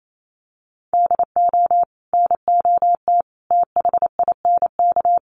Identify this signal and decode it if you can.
Morse code, heard as a keyed tone, about 28 wpm, 700 Hz. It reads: DO NOT THINK
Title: Sets of 5 Words Encore2 28wpm